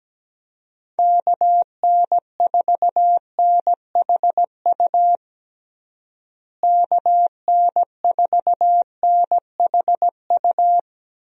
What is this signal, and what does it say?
Morse code, heard as a keyed tone, about 17 wpm, 700 Hz. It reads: KN4NHU KN4NHU
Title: USA Call Signs - 6 Characters - Repeated 2x - 17wpm